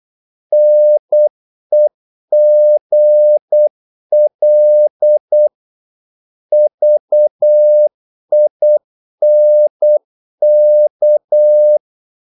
Morse code, heard as a keyed tone, about 8 wpm, 600 Hz. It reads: NEGL VINK
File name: Korte ord DK 8wpm